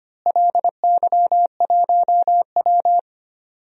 Morse code, heard as a keyed tone, about 25 wpm, 700 Hz. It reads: LY1W
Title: DXCC-100 Call Signs - 4 Characters - Repeated 3x - 25wpm